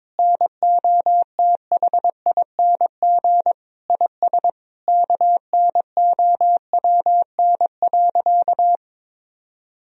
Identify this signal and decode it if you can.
Morse code, heard as a keyed tone, about 22 wpm, 700 Hz. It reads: NOTHING IS KNOWN.
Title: Sentences from Top 1000 Words - Review Part 4 - 22wpm